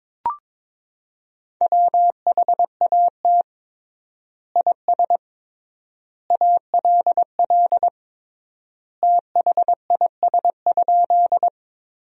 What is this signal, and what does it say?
Morse code, heard as a keyed tone, about 22 wpm, 700 Hz. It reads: WHAT IS ALL THIS?
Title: 3x Word-Spacing - Sentences from Top 100 Words - 22wpm